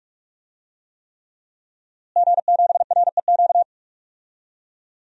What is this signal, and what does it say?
Morse code, heard as a keyed tone, about 48 wpm, 700 Hz. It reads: G7REQ
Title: DXCC-100 Call Signs - 5 Characters - Repeated 3x - 45wpm